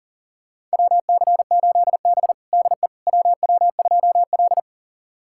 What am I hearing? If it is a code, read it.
Morse code, heard as a keyed tone, about 40 wpm, 700 Hz. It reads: WC8B DE WW2L
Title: USA Callsign DE Callsign - 4 Characters - 40wpm